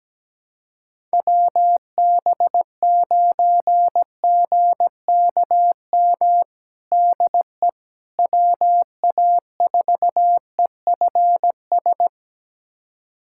Morse code, heard as a keyed tone, about 17 wpm, 700 Hz. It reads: WB9GKM DE WA4EFS
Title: USA Callsign DE Callsign - 6 Characters - 17wpm